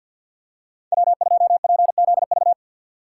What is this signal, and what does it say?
Morse code, heard as a keyed tone, about 53 wpm, 700 Hz. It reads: W2PZV
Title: USA Call Signs - 5 Characters - 2x Character Spacing - 50wpm